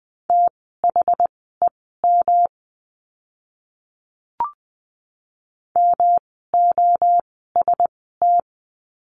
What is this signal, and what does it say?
Morse code, heard as a keyed tone, about 20 wpm, spaced as for 14 wpm, 700 Hz. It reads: THEM MOST
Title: 139 Lesson 23f - Letter f - Words 20wpm